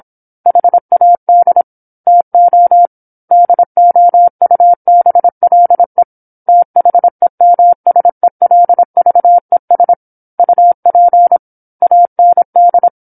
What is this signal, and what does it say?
Morse code, heard as a keyed tone, about 27 wpm, 700 Hz. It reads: HAD TO DOUBLE THEMSELVES UP AND
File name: 08-Chapter-08 26 WPM